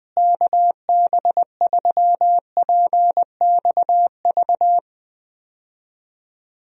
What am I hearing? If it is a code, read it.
Morse code, heard as a keyed tone, about 20 wpm, 700 Hz. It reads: KB3PXV
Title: USA Call Signs - 6 Characters - Repeated 3x - 20wpm